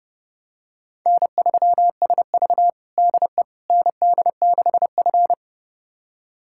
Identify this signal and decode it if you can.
Morse code, heard as a keyed tone, about 30 wpm, 700 Hz. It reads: N3SV DE ND6F
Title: USA Callsign DE Callsign - 4 Characters - 30wpm